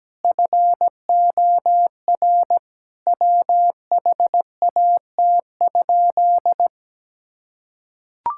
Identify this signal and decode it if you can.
Morse code, heard as a keyed tone, about 17 wpm, 700 Hz. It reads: FOR WHAT?
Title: Sentences from Top 1000 Words - Review Part 6 - 17wpm